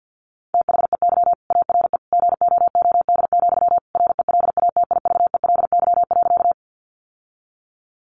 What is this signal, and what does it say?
Morse code, heard as a keyed tone, about 49 wpm, 700 Hz. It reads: THEY ARE GOOD, RELATIVELY.
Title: Sentences from Top 1100 Words 50wpm